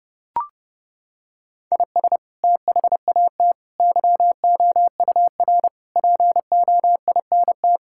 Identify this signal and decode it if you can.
Morse code, heard as a keyed tone, about 32 wpm, 700 Hz. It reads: IS THAT YOUR POINT
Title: Sentences from Top 1000 Words - Review Part 6 - 30wpm